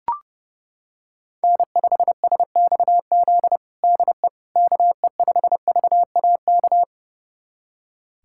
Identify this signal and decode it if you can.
Morse code, heard as a keyed tone, about 30 wpm, 700 Hz. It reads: N5SXZ DE KE5VAK
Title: USA Callsign DE Callsign 30wpm